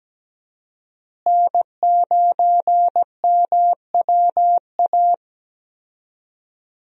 Morse code, heard as a keyed tone, about 17 wpm, 700 Hz. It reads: N9MWA
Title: USA Call Signs - 5 Characters - Repeated 3x - 17wpm